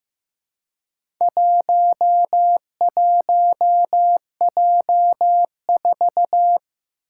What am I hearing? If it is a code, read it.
Morse code, heard as a keyed tone, about 15 wpm, 700 Hz. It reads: 11J4
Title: Four Challenging Letter-Number Combinations 15wpm